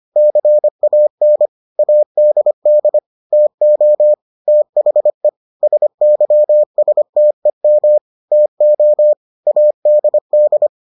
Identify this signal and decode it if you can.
Morse code, heard as a keyed tone, about 25 wpm, 600 Hz. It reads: CAN ADD TO THE SYSTEM TO ADD